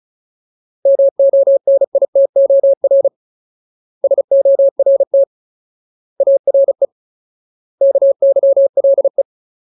Morse code, heard as a keyed tone, about 34 wpm, 550 Hz. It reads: MONITOR SORT ARE KYLE
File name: Ultimate Word List / 35WPM Ultimate